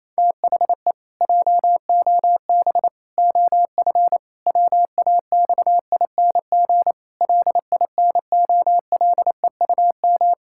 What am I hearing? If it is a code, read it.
Morse code, heard as a keyed tone, about 28 wpm, 700 Hz. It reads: THE JOB OF WAXING LINOLEUM
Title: Pangram Challenge v2 - 28wpm